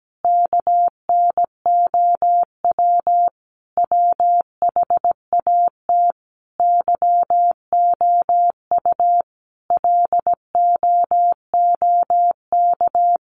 Morse code, heard as a keyed tone, about 17 wpm, 700 Hz. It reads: KNOW WHAT YOU LOOK
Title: Sentences from Top 100 Words 17wpm